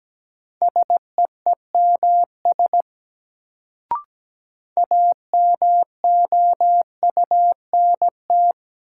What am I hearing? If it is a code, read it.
Morse code, heard as a keyed tone, about 17 wpm, 700 Hz. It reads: SEEMS AMOUNT
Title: Sets of 3 Words - New Word Drill - 17wpm